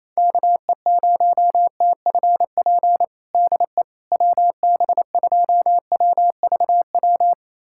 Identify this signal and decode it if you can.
Morse code, heard as a keyed tone, about 28 wpm, 700 Hz. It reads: KE0TFP DE WB2WVW
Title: USA Callsign DE Callsign - 6 Characters - 28wpm